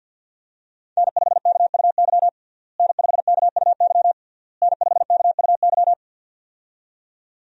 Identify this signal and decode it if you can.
Morse code, heard as a keyed tone, about 53 wpm, 700 Hz. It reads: N5KUY N5KUY N5KUY
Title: USA Call Signs - 5 Characters - Repeated 3x - 50wpm